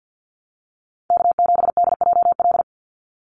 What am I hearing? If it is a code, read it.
Morse code, heard as a keyed tone, about 50 wpm, 700 Hz. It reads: K7DJL
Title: NA SOTA Chaser Callsigns 50wpm